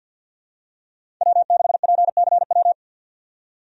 Morse code, heard as a keyed tone, about 52 wpm, 700 Hz. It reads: W6PCW
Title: USA Call Signs - 5 Characters - 50wpm